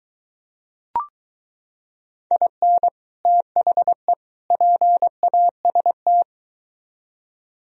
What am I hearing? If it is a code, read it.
Morse code, heard as a keyed tone, about 23 wpm, 700 Hz. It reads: IN THE PAST